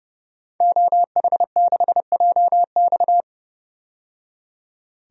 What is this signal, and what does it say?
Morse code, heard as a keyed tone, about 30 wpm, 700 Hz. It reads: OH6JX
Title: DXCC-100 Call Signs - 5 Characters - 30wpm